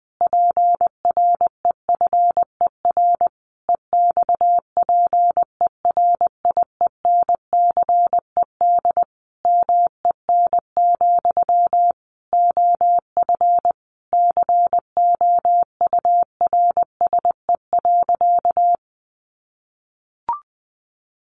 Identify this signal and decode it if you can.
Morse code, heard as a keyed tone, about 20 wpm, 700 Hz. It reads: PREFER EXPERIENCED MEN, OF COURSE.
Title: Sentences from Top 1100 Words 20wpm